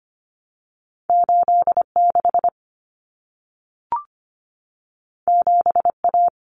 Morse code, heard as a keyed tone, about 25 wpm, 700 Hz. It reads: 86 7A
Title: Common Cut Numbers - 2-Digit 25wpm